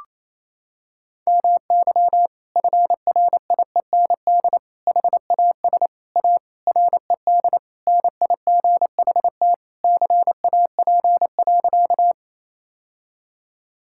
Morse code, heard as a keyed tone, about 28 wpm, 700 Hz. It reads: MY FRIEND HAS A RED NIGHT CAP.
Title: Sentences from Top 1600 Words 28wpm